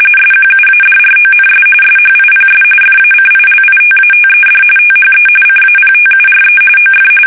RTTY, about 45.45 baud. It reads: SQUADRONS SHIPS FACILITATED THE CAPTURE OF
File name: fsk850 / usnavy2